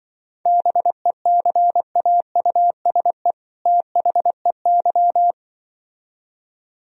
Morse code, heard as a keyed tone, about 24 wpm, 700 Hz. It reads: BECAUSE THEY
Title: Speed-Racing - Sets of 2 Words 20wpm